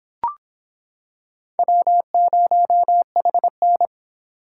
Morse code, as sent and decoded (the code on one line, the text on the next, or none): .-- ----- .... -.
W0HN